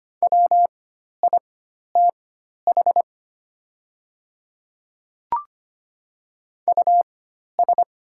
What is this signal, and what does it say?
Morse code, heard as a keyed tone, about 25 wpm, spaced as for 12 wpm, 700 Hz. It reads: WITH US
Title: 4x Character Spacing - Top 300 Words - 25wpm